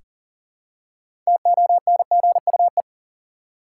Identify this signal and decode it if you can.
Morse code, heard as a keyed tone, about 40 wpm, 700 Hz. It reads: TONGUE
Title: Six Letter Words - 40wpm